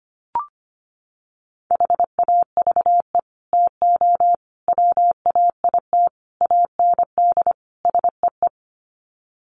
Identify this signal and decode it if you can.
Morse code, heard as a keyed tone, about 25 wpm, 700 Hz. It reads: HAVE TO WAIT AND SEE
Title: Sets of 5 Words Encore2 25wpm